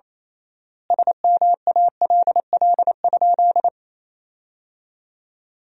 Morse code, heard as a keyed tone, about 28 wpm, 700 Hz. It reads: SMALL?
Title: 3x Word-Spacing - Sentences from Top 200 Words - 28wpm